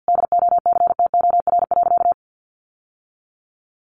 Morse code, heard as a keyed tone, about 50 wpm, 700 Hz. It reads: DOCTOR.
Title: Sentences from Top 1500 Words 50wpm